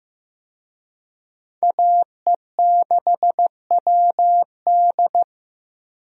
Morse code, heard as a keyed tone, about 15 wpm, 700 Hz. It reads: AE6WD
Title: USA Call Signs - 5 Characters - Repeated 3x - 15wpm